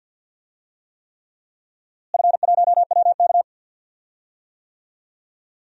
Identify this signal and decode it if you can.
Morse code, heard as a keyed tone, about 53 wpm, 700 Hz. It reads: F1WK
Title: DXCC-100 Call Signs - 4 Characters - Repeated 3x - 50wpm